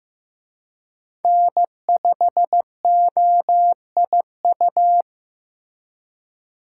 Morse code, heard as a keyed tone, about 18 wpm, 700 Hz. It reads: N5OIU
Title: Speed Racing - USA 1x3 Call Signs 15wpm